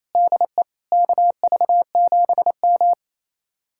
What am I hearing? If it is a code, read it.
Morse code, heard as a keyed tone, about 35 wpm, 700 Hz. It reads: DE KV7M
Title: Speed-Racing Common QSO Phrases Encore - 28wpm